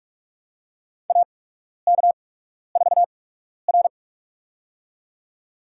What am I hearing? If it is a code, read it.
Morse code, heard as a keyed tone, about 46 wpm, spaced as for 12 wpm, 700 Hz. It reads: AK4R
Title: USA Call Signs - 4 Characters - 8x Character Spacing - 45wpm